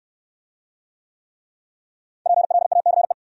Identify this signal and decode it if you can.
Morse code, heard as a keyed tone, about 65 wpm, 700 Hz. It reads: PLACE